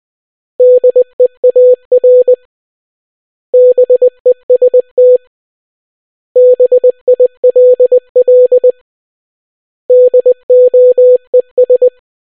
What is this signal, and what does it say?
Morse code, heard as a keyed tone, about 20 wpm, 500 Hz. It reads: DEAR BEST BILL DOES